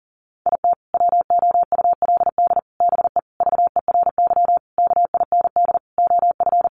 Morse code, heard as a keyed tone, about 40 wpm, 700 Hz. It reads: IT WOULD BE VERY KIND OF
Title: Sentences from Top 200 Words 40wpm